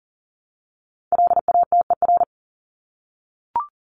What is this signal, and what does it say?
Morse code, heard as a keyed tone, about 40 wpm, 700 Hz. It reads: LATER
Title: Top 501-750 Words 40wpm